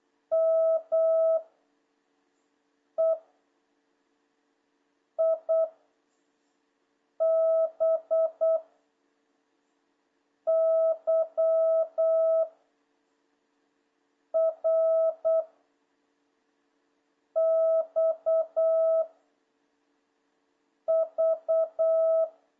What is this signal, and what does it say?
Morse code, heard as a keyed tone, about 8 wpm, 1300 Hz. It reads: M E I B Y R X V
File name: jh8cbh / 2026cw